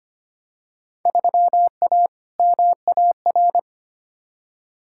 Morse code, heard as a keyed tone, about 25 wpm, 700 Hz. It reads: 3A MAR
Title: Field Day Exchanges Encore - 25wpm